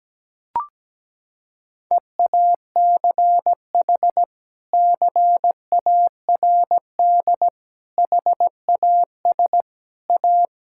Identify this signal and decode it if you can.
Morse code, heard as a keyed tone, about 17 wpm, 700 Hz. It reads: EACH CARD HAS A
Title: Sentences from Top 1000 Words - Review Part 6 - 17wpm